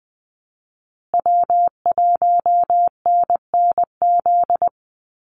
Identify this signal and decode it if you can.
Morse code, heard as a keyed tone, about 20 wpm, 700 Hz. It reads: W1NNZ
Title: USA Call Signs Repeated 3x 20wpm